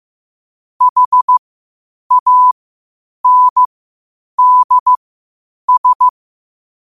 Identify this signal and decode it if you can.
Morse code, heard as a keyed tone, about 15 wpm, spaced as for 8 wpm, 1000 Hz. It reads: HANDS
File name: qotdfiles / 8wpm